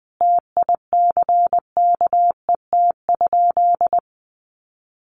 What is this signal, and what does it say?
Morse code, heard as a keyed tone, about 20 wpm, 700 Hz. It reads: TICKET?
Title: Sentences from Top 1400 Words 20wpm